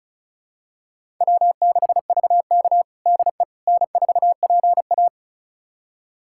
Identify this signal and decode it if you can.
Morse code, heard as a keyed tone, about 36 wpm, 700 Hz. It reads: W6VK DE N4PA
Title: USA Callsign DE Callsign - 4 Characters - 35wpm